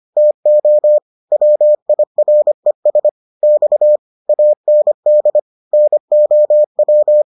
Morse code, heard as a keyed tone, about 25 wpm, 600 Hz. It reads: TO WIRES X AND NOW